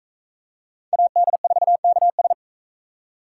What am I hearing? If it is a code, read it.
Morse code, heard as a keyed tone, about 44 wpm, 700 Hz. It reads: AD4KS